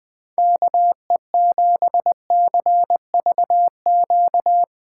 Morse code, heard as a keyed tone, about 22 wpm, 700 Hz. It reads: KE7CVQ
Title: Speed Racing - USA 2x3 Call Signs 20wpm